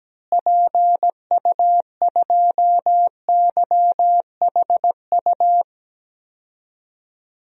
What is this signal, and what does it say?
Morse code, heard as a keyed tone, about 17 wpm, 700 Hz. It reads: PU2YHU
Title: DXCC-100 Call Signs - 6 Characters - Repeated 2x - 17wpm